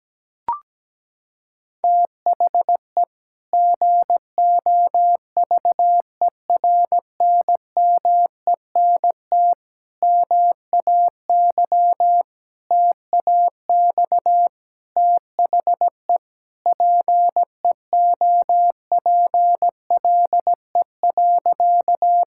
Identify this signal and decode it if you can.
Morse code, heard as a keyed tone, about 17 wpm, 700 Hz. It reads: THE GOVERNMENT MAY TAX THE PEOPLE.
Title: Sentences from Top 1000 Words - Review Part 5 - 17wpm